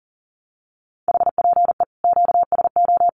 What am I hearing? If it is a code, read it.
Morse code, heard as a keyed tone, about 40 wpm, 700 Hz. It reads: HPE QSO
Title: Common QSO Phrases 40wpm